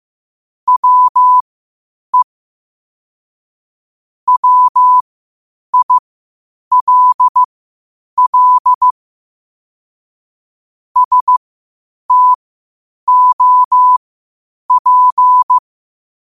Morse code, heard as a keyed tone, about 15 wpm, spaced as for 8 wpm, 1000 Hz. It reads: WE WILL STOP
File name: qotdfiles / 8wpm